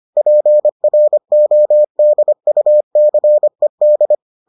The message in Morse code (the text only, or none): PRODUCED